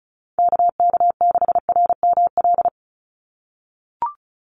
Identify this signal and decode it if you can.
Morse code, heard as a keyed tone, about 34 wpm, 700 Hz. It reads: KK6RML
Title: USA Call Signs 35wpm